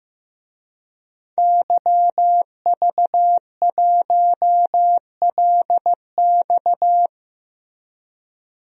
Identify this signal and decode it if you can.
Morse code, heard as a keyed tone, about 15 wpm, 700 Hz. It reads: YV1LX
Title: DXCC-100 Call Signs - 5 Characters - Repeated 2x - 15wpm